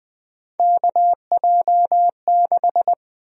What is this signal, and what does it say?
Morse code, heard as a keyed tone, about 20 wpm, 700 Hz. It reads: KJ6
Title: USA Call Sign Prefixes - Repeated 2x - 20wpm